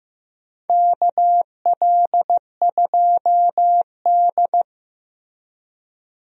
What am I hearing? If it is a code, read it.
Morse code, heard as a keyed tone, about 15 wpm, 700 Hz. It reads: KL2D
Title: USA Call Signs - 4 Characters - Repeated 2x - 15wpm